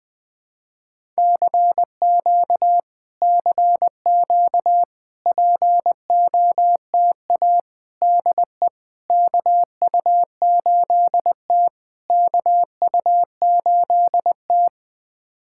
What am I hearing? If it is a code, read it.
Morse code, heard as a keyed tone, about 20 wpm, 700 Hz. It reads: CQ CQ POTA DE KU8T KU8T
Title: POTA QSO Practice - 20wpm